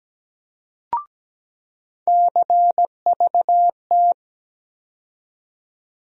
Morse code, heard as a keyed tone, about 17 wpm, 700 Hz. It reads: CVT
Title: USA Call Sign Postfixes - 3 Letters 17wpm